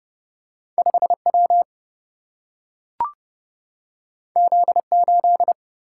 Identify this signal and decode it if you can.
Morse code, heard as a keyed tone, about 30 wpm, 700 Hz. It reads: 5 W Z 8
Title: Single-Letter Single-Number 30wpm